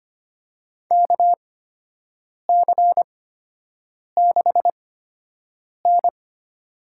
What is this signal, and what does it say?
Morse code, heard as a keyed tone, about 25 wpm, spaced as for 7 wpm, 700 Hz. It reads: KC6N
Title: USA Call Signs - 4 Characters - 8x Character Spacing - 25wpm